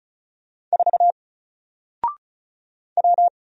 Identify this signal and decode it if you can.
Morse code, heard as a keyed tone, about 36 wpm, 700 Hz. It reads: 4 W
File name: Single-Letter-Number-35wpm